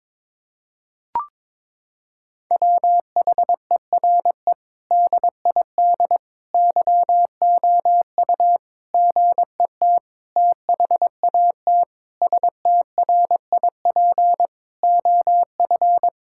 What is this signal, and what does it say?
Morse code, heard as a keyed tone, about 22 wpm, 700 Hz. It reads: WHERE DID YOU GET THAT STRIP OF
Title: Sentences from Top 1600 Words 22wpm